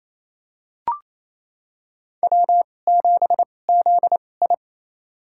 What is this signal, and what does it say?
Morse code, heard as a keyed tone, about 28 wpm, spaced as for 20 wpm, 690 Hz. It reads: W7ZI
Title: USA Call Signs - 4 Characters - 2x Character Spacing - 28wpm